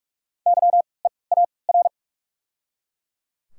Morse code, heard as a keyed tone, about 41 wpm, 690 Hz. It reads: Y E A R
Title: 3x Character Spacing - Top 200 Words - 45wpm